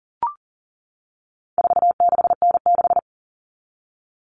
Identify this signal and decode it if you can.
Morse code, heard as a keyed tone, about 40 wpm, 700 Hz. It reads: 46N6
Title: Common Cut Numbers - 4-Digit 40wpm